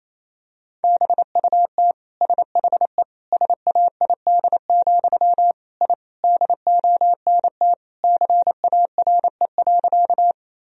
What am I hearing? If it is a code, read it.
Morse code, heard as a keyed tone, about 28 wpm, 700 Hz. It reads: BUT SHE SAID, I DONT CARE.
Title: Sentences from Top 1000 Words - Review Part 1 - 28wpm